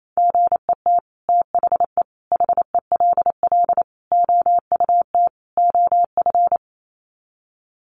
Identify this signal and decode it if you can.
Morse code, heard as a keyed tone, about 28 wpm, 700 Hz. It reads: GET THE HELL OUT OF
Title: Sets of 5 Words Encore2 28wpm